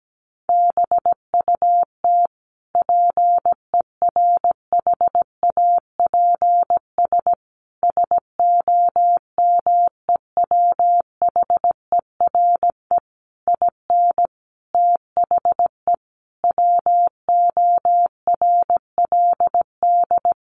Morse code, heard as a keyed tone, about 17 wpm, 700 Hz. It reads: BUT PERHAPS SOMEWHERE IN THE WORLD
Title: Sentences from Top 1400 Words 17wpm